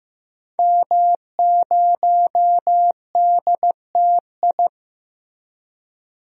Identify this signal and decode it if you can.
Morse code, heard as a keyed tone, about 15 wpm, 700 Hz. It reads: M0DTI
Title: DXCC-100 Call Signs - 5 Characters - Repeated 3x - 15wpm